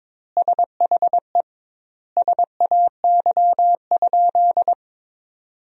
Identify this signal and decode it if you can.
Morse code, heard as a keyed tone, about 22 wpm, 700 Hz. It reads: SHE SAY?
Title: 2x Word-Spacing - Sentences from Top 200 Words - 22wpm